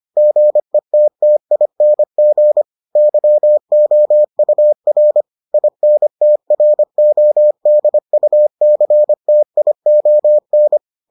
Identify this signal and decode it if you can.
Morse code, heard as a keyed tone, about 25 wpm, 600 Hz. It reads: GETTING YOUR INTRODUCTION